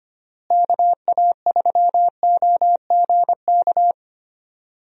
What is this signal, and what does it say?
Morse code, heard as a keyed tone, about 25 wpm, 700 Hz. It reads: KA3OGK
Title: USA Call Signs - 6 Characters - Repeated 3x - 25wpm